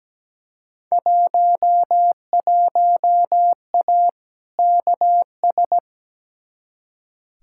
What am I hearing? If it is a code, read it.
Morse code, heard as a keyed tone, about 17 wpm, 700 Hz. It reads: 11A KS
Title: Field Day Exchanges Encore - 17wpm